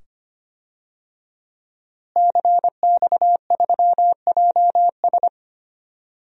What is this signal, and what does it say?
Morse code, heard as a keyed tone, about 25 wpm, 700 Hz. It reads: CX3JS
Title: DXCC-100 Call Signs - 5 Characters - Repeated 2x - 25wpm